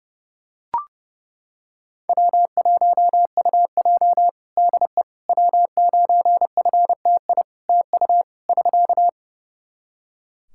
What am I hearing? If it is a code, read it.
Morse code, heard as a keyed tone, about 30 wpm, 700 Hz. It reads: W1UJ DE W9FTI TU <SK>
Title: Practice QSO - Exchange 4 - Ending - 30wpm